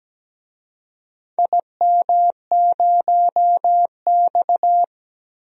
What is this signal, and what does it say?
Morse code, heard as a keyed tone, about 17 wpm, 700 Hz. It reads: IM0X
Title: DXCC-100 Call Signs - 4 Characters - 17wpm